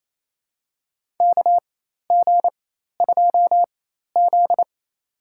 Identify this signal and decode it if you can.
Morse code, heard as a keyed tone, about 28 wpm, spaced as for 13 wpm, 700 Hz. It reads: KG2Z
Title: USA Call Signs - 4 Characters - 4x Character Spacing - 28wpm